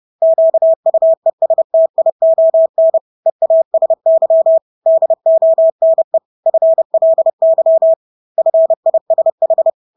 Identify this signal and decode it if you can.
Morse code, heard as a keyed tone, about 30 wpm, 650 Hz. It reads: QUESTION EASY DONE FLY FISH
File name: Engelske ord 30-30 wpm